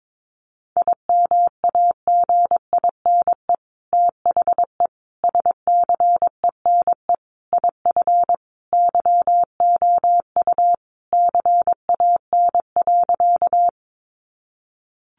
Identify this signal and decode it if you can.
Morse code, heard as a keyed tone, about 22 wpm, 700 Hz. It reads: IMAGINE THE SCENE IF YOU CAN.
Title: Sentences from Top 1400 Words 22wpm